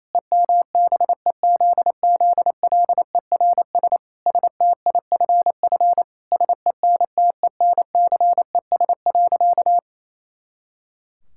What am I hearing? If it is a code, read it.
Morse code, heard as a keyed tone, about 28 wpm, 700 Hz. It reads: EMBEZZLERS STIFF SENTENCES.
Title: Pangram Challenge v2 - 28wpm